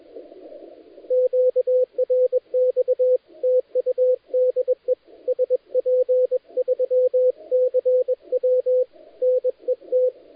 Morse code, heard as a keyed tone, about 21 wpm, 510 Hz, with radio noise. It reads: QRX TU DE SP3CW NET
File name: dlazaawansowanych / nagraniazpasma / titawaNETcontrol